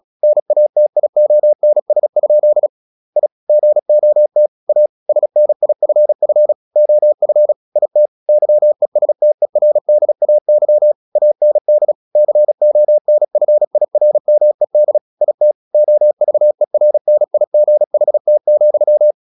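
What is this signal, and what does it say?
Morse code, heard as a keyed tone, about 36 wpm, 600 Hz. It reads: NATIONS? I GOT A SNIFF OF IT YESTERDAY AND CONFIRMED IT OVERNIGHT,
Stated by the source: Morse practice at 36 wpm, text of Ken Follett's novel Never, from CW med Gnister.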